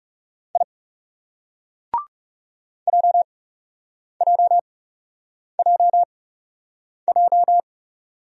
Morse code, heard as a keyed tone, about 38 wpm, spaced as for 8 wpm, 700 Hz. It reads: I JJJJ